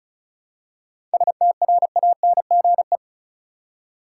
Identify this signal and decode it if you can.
Morse code, heard as a keyed tone, about 34 wpm, 700 Hz. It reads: STRANGE
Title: Seven Letter Words - 35wpm